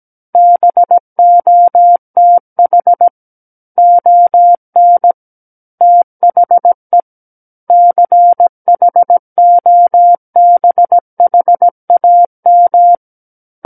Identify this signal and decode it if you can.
Morse code, heard as a keyed tone, about 17 wpm, 700 Hz. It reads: BOTH ON THE CHOBHAM
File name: Book One / 08-Chapter 8 - 17 WPM